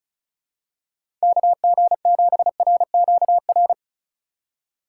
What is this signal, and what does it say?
Morse code, heard as a keyed tone, about 34 wpm, 700 Hz. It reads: KG7RQR
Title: USA Call Signs - 6 Characters - 4x Character Spacing - 35wpm